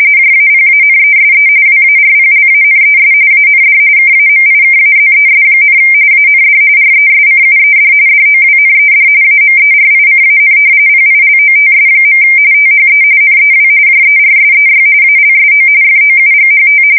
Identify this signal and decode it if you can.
RTTY, about 45.45 baud. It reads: JAPANESE WARPLANES AND KAMIKAZES, AND RESCUED DOWNED AMERICAN PILOTS. THE NEW JERSEY ALSO FOUGHT
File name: fsk170 / battleshipnj